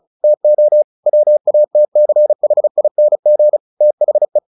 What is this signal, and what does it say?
Morse code, heard as a keyed tone, about 34 wpm, 600 Hz. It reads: TO WATCHING THE
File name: the old man and the sea / the old man and the sea 1 35wpm